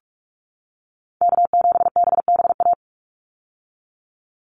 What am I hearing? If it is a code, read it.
Morse code, heard as a keyed tone, about 44 wpm, 700 Hz. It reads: K7BBA
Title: USA Call Signs 45wpm